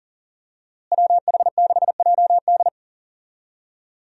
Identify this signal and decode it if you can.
Morse code, heard as a keyed tone, about 37 wpm, 700 Hz. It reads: WH6JD
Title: USA Call Signs - 5 Characters - Repeated 3x - 40wpm